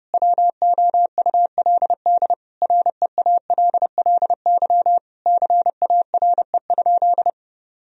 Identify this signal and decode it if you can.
Morse code, heard as a keyed tone, about 30 wpm, 700 Hz. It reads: WOULD REALLY CARE?
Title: Sentences from Top 1000 Words - Review Part 3 - 30wpm